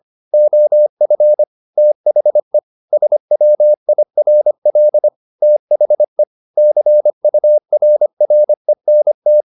Morse code, heard as a keyed tone, about 25 wpm, 600 Hz. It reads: OF THE SWIRL THE CURRENT
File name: the old man and the sea part 1 25wpm